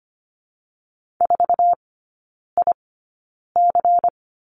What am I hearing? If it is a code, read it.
Morse code, heard as a keyed tone, about 25 wpm, spaced as for 9 wpm, 700 Hz. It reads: 4IC
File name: course / 25:10wpm / 093 Lesson 16e - Letter c - Three Characters 25wpm